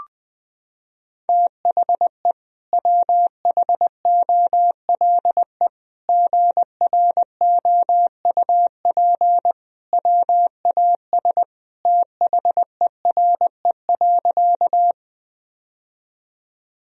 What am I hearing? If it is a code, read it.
Morse code, heard as a keyed tone, about 20 wpm, 700 Hz. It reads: THE WHOLE GROUP WAS THERE.
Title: Sentences from Top 1000 Words - Review Part 1 - 20wpm